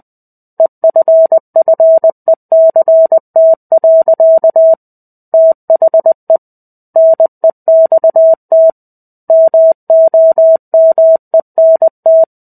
Morse code, heard as a keyed tone, about 20 wpm, 650 Hz. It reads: EFFECT. THE NEXT MOMENT
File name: 05-Chapter-05 20 WPM